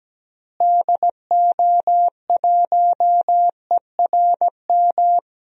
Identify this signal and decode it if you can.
Morse code, heard as a keyed tone, about 17 wpm, 700 Hz. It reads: DO1ERM
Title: DXCC-100 Call Signs - 6 Characters - Repeated 3x - 17wpm